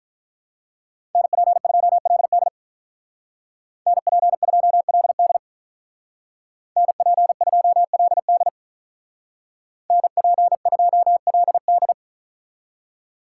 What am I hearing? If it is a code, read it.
Morse code, heard as a keyed tone, about 44 wpm, 700 Hz. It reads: NP2LD NP2LD NP2LD NP2LD
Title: Speed Racing - USA Call Signs 35wpm